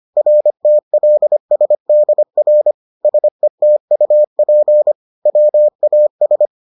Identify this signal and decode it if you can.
Morse code, heard as a keyed tone, about 25 wpm, 600 Hz. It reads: RTLSDR SETUP WAS